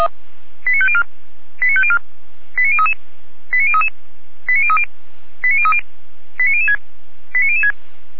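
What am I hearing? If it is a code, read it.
Five-tone selective call (ZVEI), heard as quick runs of five tones, tones of about 70 ms. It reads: 89504 89504 89033 89033 89030 89030 89006 89006
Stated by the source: ZVEI five-tone selective call received on SDR, from the fire brigade audio archive audio.f85360.feuerwehr-orlamuende.org (Feuerwehr Orlamünde).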